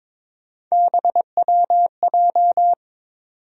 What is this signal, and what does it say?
Morse code, heard as a keyed tone, about 22 wpm, 700 Hz. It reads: BWJ
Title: USA Call Sign Postfixes - 3 Letters 22wpm